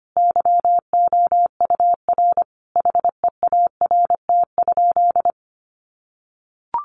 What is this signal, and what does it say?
Morse code, heard as a keyed tone, about 25 wpm, 700 Hz. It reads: YOUR HEART?
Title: Sentences from Top 1400 Words 25wpm